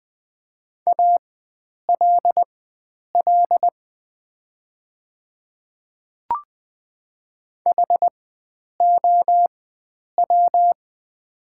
Morse code, heard as a keyed tone, about 20 wpm, spaced as for 9 wpm, 700 Hz. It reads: ALL HOW
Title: 4x Character Spacing - Top 100 Words - 20wpm